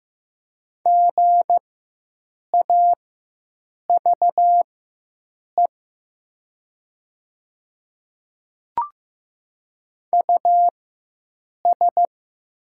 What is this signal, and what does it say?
Morse code, heard as a keyed tone, about 15 wpm, spaced as for 7 wpm, 700 Hz. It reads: GAVE US
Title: 4x Character Spacing - Top 500 Words - 15wpm